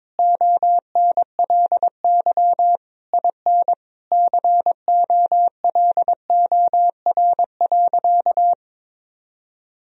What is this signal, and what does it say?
Morse code, heard as a keyed tone, about 22 wpm, 700 Hz. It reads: ONLY IN COLOR.
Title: Sentences from Top 1000 Words - Review Part 2 - 22wpm